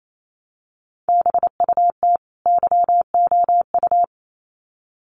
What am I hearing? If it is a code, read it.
Morse code, heard as a keyed tone, about 28 wpm, 700 Hz. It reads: BUT YOU
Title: Sets of 2 Words 28wpm